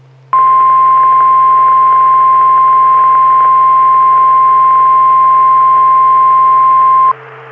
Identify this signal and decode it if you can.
Emergency Alert System (SAME) tones (the 1050 Hz attention tone). It is heard from 0.3 to 7.1 s.
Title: Saline, AK SKYWARN Repeater System - W5RHS